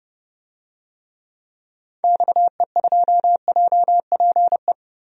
Morse code, heard as a keyed tone, about 30 wpm, 700 Hz. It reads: XE2JPE
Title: DXCC-100 Call Signs - 6 Characters - Repeated 2x - 30wpm